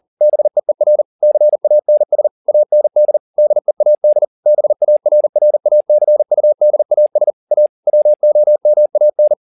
Morse code, heard as a keyed tone, about 40 wpm, 600 Hz. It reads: BEER CANS AND DEAD BARRACUDAS A WOMAN
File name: the old man and the sea 2 40wpm